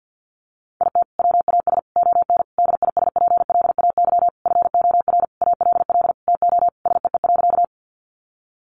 Morse code, heard as a keyed tone, about 50 wpm, 700 Hz. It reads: IT WAS ON DISPLAY FOR ALL TO SEE.
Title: Sentences from Top 1200 Words 50wpm